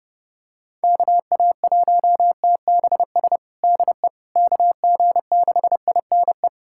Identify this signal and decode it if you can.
Morse code, heard as a keyed tone, about 30 wpm, 700 Hz. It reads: KA1TBS DE KG6INE
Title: USA Callsign DE Callsign 30wpm